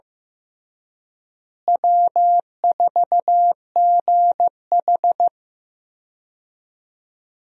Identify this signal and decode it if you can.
Morse code, heard as a keyed tone, about 15 wpm, 700 Hz. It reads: W4GH
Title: USA Call Signs - 4 Characters - Repeated 3x - 15wpm